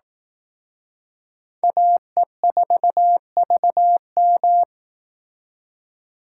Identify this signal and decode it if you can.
Morse code, heard as a keyed tone, about 18 wpm, 700 Hz. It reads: AE4VM
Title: Speed Racing - USA 2x2 Call Signs 15wpm